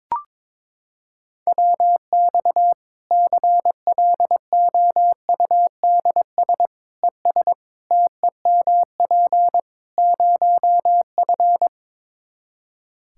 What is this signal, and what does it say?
Morse code, heard as a keyed tone, about 22 wpm, 700 Hz. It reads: WX CLOUDS ES TEMP 0F
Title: Practice QSO - Exchange 2 - Info - 22wpm